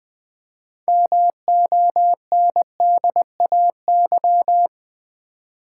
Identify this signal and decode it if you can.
Morse code, heard as a keyed tone, about 20 wpm, 700 Hz. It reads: MONDAY
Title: 6 Letter Words - First 3 Letters Given - 20wpm